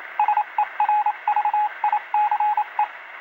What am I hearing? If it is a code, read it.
Morse code, heard as a keyed tone, about 27 wpm, 840 Hz, with radio noise. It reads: SERVICE